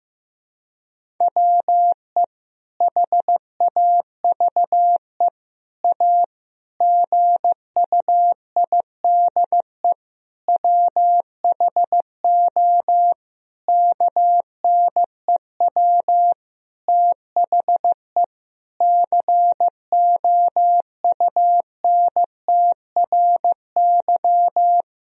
Morse code, heard as a keyed tone, about 15 wpm, 700 Hz. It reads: WE HAVE A GUIDE WHO KNEW THE COUNTRY
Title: Sentences from Top 1000 Words - Review Part 3 - 15wpm